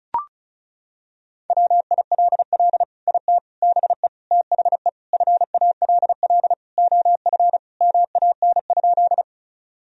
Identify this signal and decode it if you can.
Morse code, heard as a keyed tone, about 36 wpm, 700 Hz. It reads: WILL IT BE THE FALL OF MAN?
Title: Sentences from Top 1000 Words - Review Part 2 - 35wpm